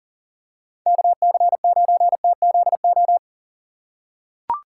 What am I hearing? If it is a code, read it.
Morse code, heard as a keyed tone, about 40 wpm, 700 Hz. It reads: KC9TZO
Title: K1USN SST QSO Practice - 40wpm